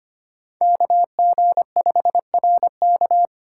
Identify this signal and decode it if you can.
Morse code, heard as a keyed tone, about 25 wpm, 700 Hz. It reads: KG5RK
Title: USA Call Signs - 8x Character Spacing - 25wpm